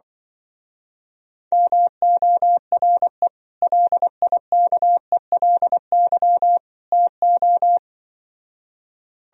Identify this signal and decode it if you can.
Morse code, heard as a keyed tone, about 24 wpm, 700 Hz. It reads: MORE LIKELY TO
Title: Speed-Racing - Sets of 3 Words 20wpm